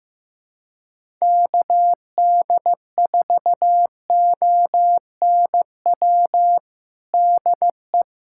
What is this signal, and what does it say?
Morse code, heard as a keyed tone, about 15 wpm, 700 Hz. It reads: KD4ONW DE
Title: USA Callsign DE Callsign - 6 Characters - 15wpm